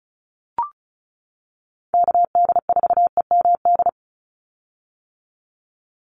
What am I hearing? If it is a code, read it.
Morse code, heard as a keyed tone, about 34 wpm, 700 Hz. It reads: KD4EMD
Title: USA Call Signs 35wpm